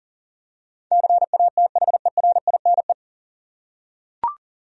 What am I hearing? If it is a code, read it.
Morse code, heard as a keyed tone, about 40 wpm, 700 Hz. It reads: CATHERINE
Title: USA Common Names - 40wpm